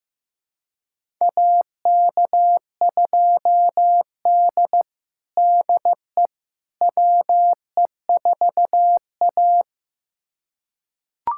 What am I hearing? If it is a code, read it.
Morse code, heard as a keyed tone, about 16 wpm, 700 Hz. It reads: AK2D DE WE4A
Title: USA Callsign DE Callsign - 4 Characters - 15wpm